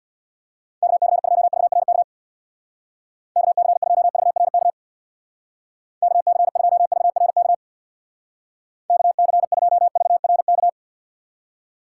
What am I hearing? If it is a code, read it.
Morse code, heard as a keyed tone, about 63 wpm, 700 Hz. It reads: KC2VRK KC2VRK KC2VRK KC2VRK